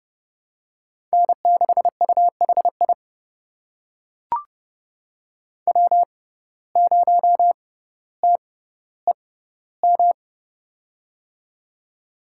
Morse code, heard as a keyed tone, about 30 wpm, spaced as for 15 wpm, 700 Hz. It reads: N6UHI W0TEM
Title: USA Call Signs - 5 Characters - 6x Character Spacing - 30wpm